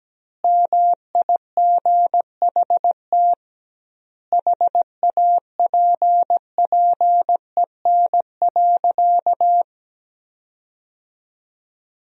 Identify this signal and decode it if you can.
Morse code, heard as a keyed tone, about 17 wpm, 700 Hz. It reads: MIGHT HAPPEN.
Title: 2x Word-Spacing - Sentences from Top 400 Words - 17wpm